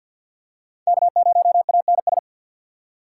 Morse code, heard as a keyed tone, about 49 wpm, 700 Hz. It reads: K0ANS
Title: USA Call Signs - 5 Characters - 2x Character Spacing - 50wpm